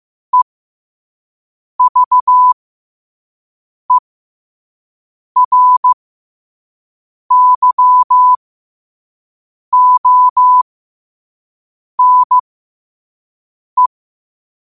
Morse code, heard as a keyed tone, about 15 wpm, spaced as for 5 wpm, 1000 Hz. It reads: EVERYONE
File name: qotdfiles / 5wpm